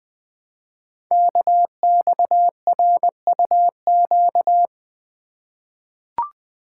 Morse code, heard as a keyed tone, about 20 wpm, 700 Hz. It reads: KXRUQ
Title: Code Groups - 5 Letters - 20wpm